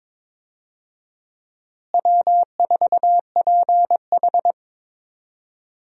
Morse code, heard as a keyed tone, about 22 wpm, 700 Hz. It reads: W4PH
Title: USA Call Signs - 4 Characters - Repeated 2x - 22wpm